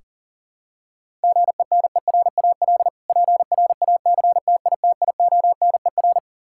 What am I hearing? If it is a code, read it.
Morse code, heard as a keyed tone, about 37 wpm, 700 Hz. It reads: GENERAL PRACTITIONER
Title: Practice QSO - Exchange 3 - More Info - 40wpm